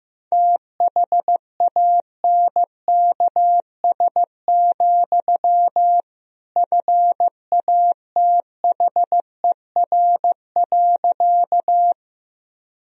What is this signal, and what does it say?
Morse code, heard as a keyed tone, about 16 wpm, 700 Hz. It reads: THANKS, FATHER.
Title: Sentences from Top 1000 Words - Review Part 3 - 15wpm